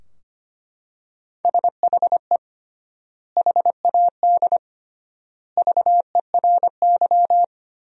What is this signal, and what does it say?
Morse code, heard as a keyed tone, about 25 wpm, 700 Hz. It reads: SHE HAD VERY
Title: 3x Word-Spacing - Sentences from Top 400 Words - 25wpm